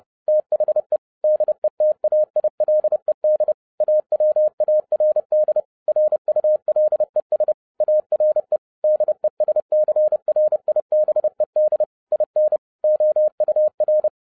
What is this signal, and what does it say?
Morse code, heard as a keyed tone, about 30 wpm, 610 Hz. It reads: THE DETAILED AWARD RULES ARE DESCRIBED IN OUR